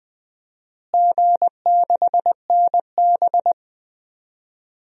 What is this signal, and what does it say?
Morse code, heard as a keyed tone, about 20 wpm, 700 Hz. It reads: G6NB
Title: DXCC-100 Call Signs - 4 Characters - Repeated 2x - 20wpm